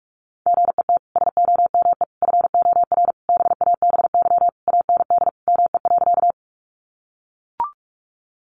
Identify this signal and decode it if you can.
Morse code, heard as a keyed tone, about 44 wpm, 700 Hz. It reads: GET SOME FOR BABY AND ME.
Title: Sentences from Top 1300 Words 45wpm